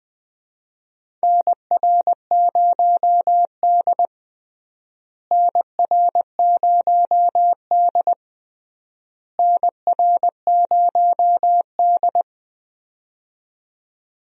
Morse code, heard as a keyed tone, about 20 wpm, 700 Hz. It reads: NR0D NR0D NR0D
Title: USA Call Signs - 4 Characters - Repeated 3x - 20wpm